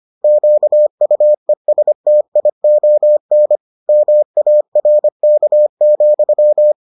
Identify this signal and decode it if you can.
Morse code, heard as a keyed tone, about 25 wpm, 600 Hz. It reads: QUESTION MARK,